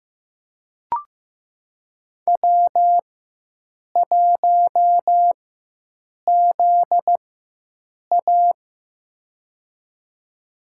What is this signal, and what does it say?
Morse code, heard as a keyed tone, about 15 wpm, spaced as for 7 wpm, 700 Hz. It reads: W1ZA
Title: USA Call Signs - 4x Character Spacing - 15wpm